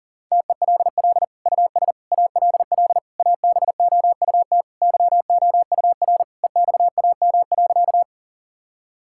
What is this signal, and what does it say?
Morse code, heard as a keyed tone, about 40 wpm, 700 Hz. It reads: TELL US ALL ABOUT YOUR EXAM.
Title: Sentences from Top 1000 Words - Review Part 5 - 40wpm